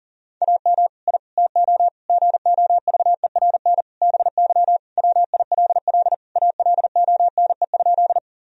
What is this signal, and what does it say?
Morse code, heard as a keyed tone, about 40 wpm, 700 Hz. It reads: AM I TO GOVERN BY WILL ALONE?
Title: Sentences from Top 1000 Words - Review Part 2 - 40wpm